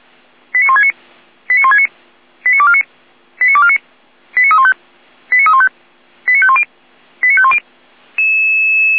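Five-tone selective call (ZVEI), heard as quick runs of five tones, tones of about 70 ms. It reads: 89169 89169 89249 89249 89315 89315 89510 89510
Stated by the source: ZVEI five-tone selective call 89167 received by SDR.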